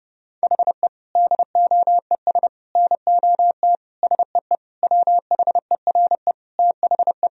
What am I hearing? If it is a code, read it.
Morse code, heard as a keyed tone, about 32 wpm, 700 Hz. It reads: HE DOES NOT SEE WHERE THE
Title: Sentences from Top 1000 Words - Review Part 1 - 30wpm